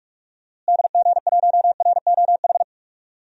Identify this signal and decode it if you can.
Morse code, heard as a keyed tone, about 42 wpm, 700 Hz. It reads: DG1ROH
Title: DXCC-100 Call Signs - 6 Characters - Repeated 2x - 45wpm